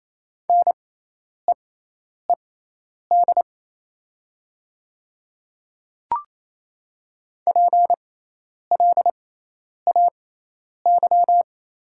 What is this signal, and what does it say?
Morse code, heard as a keyed tone, about 28 wpm, spaced as for 10 wpm, 700 Hz. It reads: NEED PLAY
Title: 6x Character Spacing - Top 200 Words - 28wpm